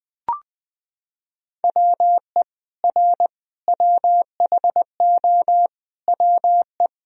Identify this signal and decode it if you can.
Morse code, heard as a keyed tone, about 21 wpm, 700 Hz. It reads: WE R WHO WE
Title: Sentences from Top 200 Words and Abbreviations 20wpm